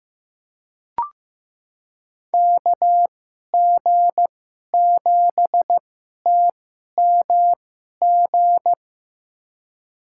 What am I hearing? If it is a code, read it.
Morse code, heard as a keyed tone, about 15 wpm, spaced as for 11 wpm, 700 Hz. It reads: KG7TMG
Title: USA Call Signs - 6 Characters - 2x Character Spacing - 15wpm